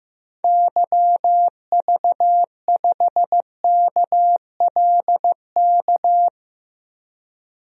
Morse code, heard as a keyed tone, about 15 wpm, 700 Hz. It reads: YV5KLK
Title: DXCC-100 Call Signs - 6 Characters - 15wpm